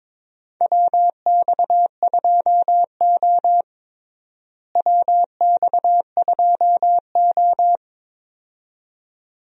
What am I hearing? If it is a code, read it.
Morse code, heard as a keyed tone, about 22 wpm, 700 Hz. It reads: WX2O WX2O
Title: USA Call Signs - 4 Characters - Repeated 2x - 22wpm